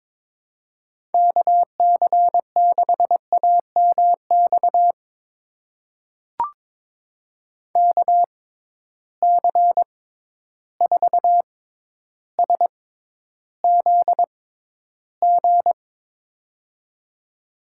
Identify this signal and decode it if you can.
Morse code, heard as a keyed tone, about 22 wpm, spaced as for 11 wpm, 700 Hz. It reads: KC6AMX KC4SZG
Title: USA Call Signs - 6 Characters - 6x Character Spacing - 22wpm